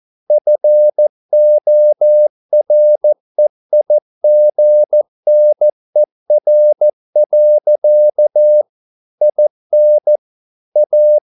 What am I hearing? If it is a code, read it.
Morse code, heard as a keyed tone, about 14 wpm, 600 Hz. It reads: FOREIGNER. IN A